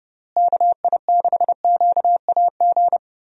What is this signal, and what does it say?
Morse code, heard as a keyed tone, about 30 wpm, 700 Hz. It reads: KI6QAG
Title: USA Call Signs - 6 Characters - Repeated 3x - 30wpm